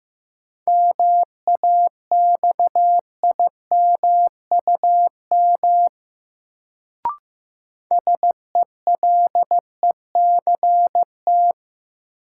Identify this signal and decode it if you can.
Morse code, heard as a keyed tone, about 15 wpm, 700 Hz. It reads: MAXIMUM SELECT
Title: Sentences from Top 1300 Words - New Word Drill - 15wpm